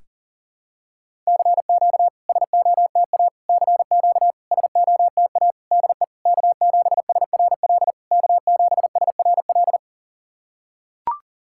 Morse code, heard as a keyed tone, about 40 wpm, 690 Hz. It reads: CQ SOTA CQ SOTA DE K7SRL K7SRL
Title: SOTA QSO Practice 40wpm